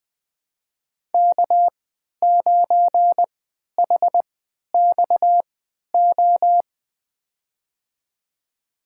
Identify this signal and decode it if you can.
Morse code, heard as a keyed tone, about 20 wpm, spaced as for 11 wpm, 700 Hz. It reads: K9HXO
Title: USA Call Signs - 5 Characters - 3x Character Spacing - 20wpm